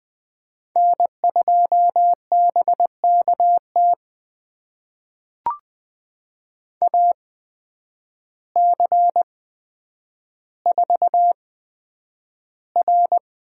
Morse code, heard as a keyed tone, about 20 wpm, spaced as for 9 wpm, 700 Hz. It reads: N2BKT AC4R
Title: USA Call Signs - 5 Characters - 8x Character Spacing - 20wpm